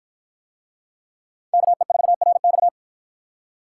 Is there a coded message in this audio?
Yes: Morse code from 1.5 to 2.7 s, heard as a keyed tone, about 48 wpm, 700 Hz. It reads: KE4RX